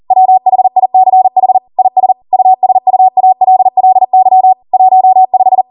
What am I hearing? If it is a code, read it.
Morse code, heard as a keyed tone, about 40 wpm, 750 Hz. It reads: WHICH IS USUALLY 15
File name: Morse / 40WPM